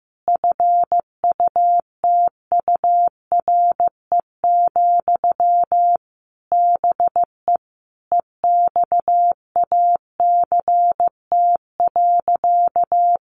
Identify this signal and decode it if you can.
Morse code, heard as a keyed tone, about 15 wpm, 700 Hz. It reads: FUTURE, BE EXACT.
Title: Sentences from Top 1300 Words 15wpm